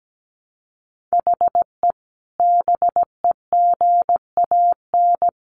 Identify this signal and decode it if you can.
Morse code, heard as a keyed tone, about 17 wpm, 700 Hz. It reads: HE BEGAN
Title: Sentences from Top 500 Words 17wpm